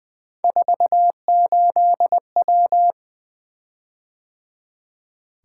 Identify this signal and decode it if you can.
Morse code, heard as a keyed tone, about 20 wpm, 700 Hz. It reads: 48W
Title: Three Letter-Number Combinations - 20wpm